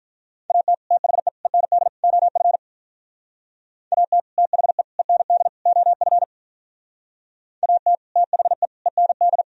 Morse code, heard as a keyed tone, about 47 wpm, 700 Hz. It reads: AT THE END OF AT THE END OF AT THE END